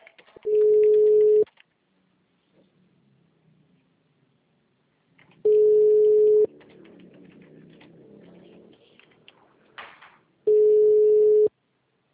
Ringback tone (Europe, 425 Hz).
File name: out- -2002- - - .248